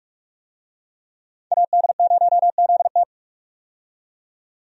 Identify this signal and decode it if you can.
Morse code, heard as a keyed tone, about 46 wpm, 700 Hz. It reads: AD0ZT
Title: USA Call Signs - 5 Characters - 45wpm